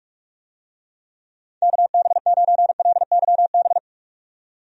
Morse code, heard as a keyed tone, about 46 wpm, 700 Hz. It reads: KB9LYB
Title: USA Call Signs - 6 Characters - 45wpm